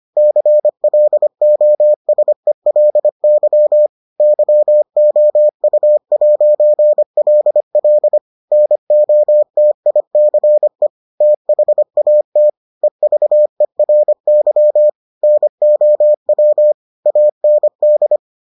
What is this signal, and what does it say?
Morse code, heard as a keyed tone, about 25 wpm, 600 Hz. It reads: CLOSELY YOU'LL NOTICE THAT EVERY NOW AND